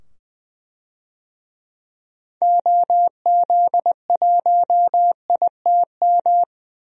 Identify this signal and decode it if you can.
Morse code, heard as a keyed tone, about 20 wpm, 700 Hz. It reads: OZ1ITM
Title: DXCC-100 Call Signs - 6 Characters - 20wpm